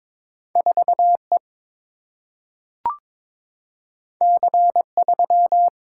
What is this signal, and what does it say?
Morse code, heard as a keyed tone, about 22 wpm, 700 Hz. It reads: 4E C3
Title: Single-Letter Single-Number 22wpm